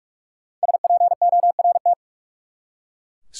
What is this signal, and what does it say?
Morse code, heard as a keyed tone, about 46 wpm, 700 Hz. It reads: SPORT
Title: Five Letter Words - 45wpm